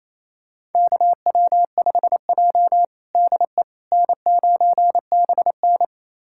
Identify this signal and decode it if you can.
Morse code, heard as a keyed tone, about 28 wpm, 700 Hz. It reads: KW5J DE N9BN
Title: USA Callsign DE Callsign - 4 Characters - 28wpm